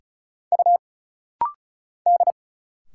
Morse code, heard as a keyed tone, about 35 wpm, 700 Hz. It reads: U D
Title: Single Letters - Rapid-Fire 35wpm